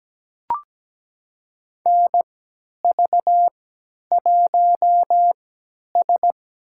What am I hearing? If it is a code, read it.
Morse code, heard as a keyed tone, about 17 wpm, spaced as for 10 wpm, 700 Hz. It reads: NV1S
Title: USA Call Signs - 4 Characters - 3x Character Spacing - 17wpm